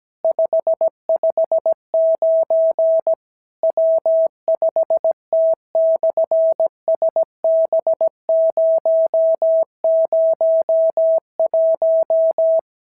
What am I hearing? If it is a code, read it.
Morse code, heard as a keyed tone, about 17 wpm, 650 Hz. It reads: 559 W5T/SB001
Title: SOTA QSO Practice 17wpm